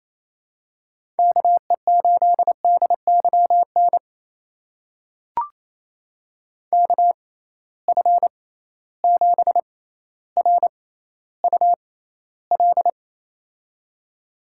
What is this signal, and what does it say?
Morse code, heard as a keyed tone, about 28 wpm, spaced as for 14 wpm, 700 Hz. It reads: KE8DYN KF7RUL
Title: USA Call Signs - 6 Characters - 6x Character Spacing - 28wpm